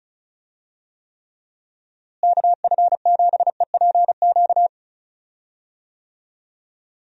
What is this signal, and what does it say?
Morse code, heard as a keyed tone, about 34 wpm, 700 Hz. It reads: KF7EPQ
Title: USA Call Signs - 6 Characters - Repeated 2x - 35wpm